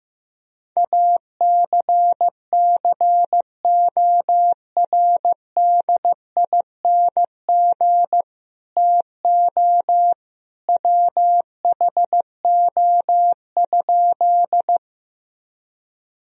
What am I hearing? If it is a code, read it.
Morse code, heard as a keyed tone, about 15 wpm, 700 Hz. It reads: ACCORDING TO WHO?
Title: Sentences from Top 1000 Words - Review Part 4 - 15wpm